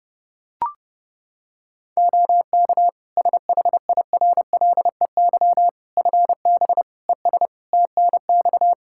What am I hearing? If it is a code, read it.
Morse code, heard as a keyed tone, about 30 wpm, 700 Hz. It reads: OK SHIRLEY FB ES TNX
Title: Practice QSO - Exchange 2 - Info - 30wpm